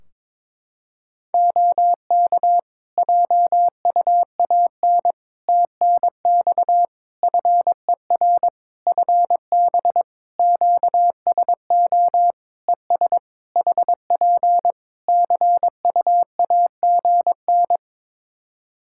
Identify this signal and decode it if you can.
Morse code, heard as a keyed tone, about 22 wpm, 700 Hz. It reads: OK JUAN TNX FER FB QSO ES HP CUAGN
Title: K1USN SST QSO Practice 22wpm